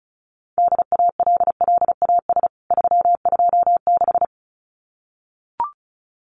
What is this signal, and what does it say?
Morse code, heard as a keyed tone, about 34 wpm, 700 Hz. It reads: DALLAS 326
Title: CWT Exchange 35wpm